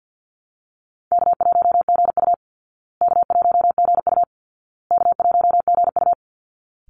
Morse code, heard as a keyed tone, about 50 wpm, 700 Hz. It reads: K1GU K1GU K1GU
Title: Easy Call Signs Repeated 3x - 50wpm